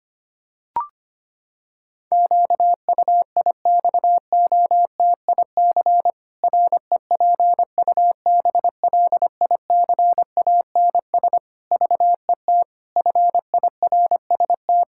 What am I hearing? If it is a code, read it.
Morse code, heard as a keyed tone, about 25 wpm, 700 Hz. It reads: QUIXOTIC REPUBLICANS VET FIRST
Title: Pangram Challenge v2 - 25wpm